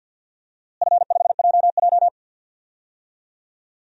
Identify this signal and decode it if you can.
Morse code, heard as a keyed tone, about 47 wpm, 700 Hz. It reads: F5JJ
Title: DXCC-100 Call Signs - 4 Characters - Repeated 3x - 50wpm